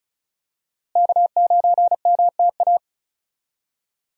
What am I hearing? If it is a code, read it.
Morse code, heard as a keyed tone, about 34 wpm, 700 Hz. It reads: K9MTA
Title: USA Call Signs - 5 Characters - Repeated 2x - 35wpm